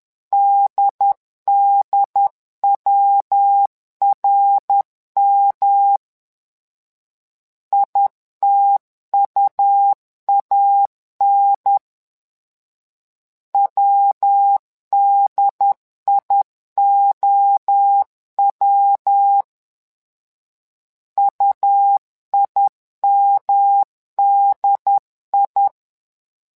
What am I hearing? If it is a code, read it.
Morse code, heard as a keyed tone, about 10 wpm, 800 Hz. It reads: DDWRM ITUAN WDIOW UIMDI
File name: LEZIONI / 3 MWRO / 3 MWRO 10wpm 2148